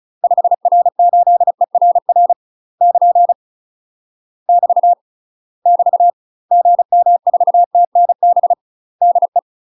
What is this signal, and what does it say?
Morse code, heard as a keyed tone, about 34 wpm, 700 Hz. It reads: 5R8ERR <KN> <BT> <BT> GM4TNB DE
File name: UK MASTER QSOs TRAINING / 35WPM / UK MASTER QSOs 0037